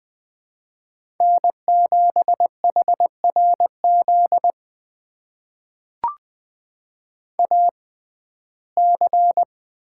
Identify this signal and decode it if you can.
Morse code, heard as a keyed tone, about 20 wpm, spaced as for 15 wpm, 700 Hz. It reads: N7HRZ AC
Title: USA Call Signs - 5 Characters - 6x Character Spacing - 20wpm